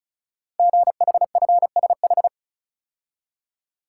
Morse code, heard as a keyed tone, about 36 wpm, 690 Hz. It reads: GHFSH
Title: Code Groups - 5 Letters - 35wpm